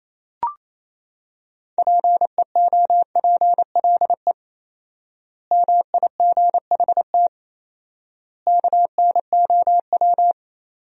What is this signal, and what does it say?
Morse code, heard as a keyed tone, about 28 wpm, 700 Hz. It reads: PEOPLE MIGHT KNOW
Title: 4x Word-Spacing - Sentences from Top 500 Words - 28wpm